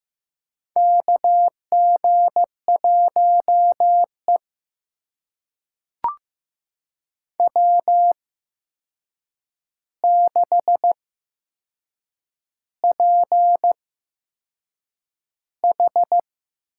Morse code, heard as a keyed tone, about 15 wpm, spaced as for 6 wpm, 700 Hz. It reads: KG1E W6PH
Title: USA Call Signs - 4 Characters - 8x Character Spacing - 15wpm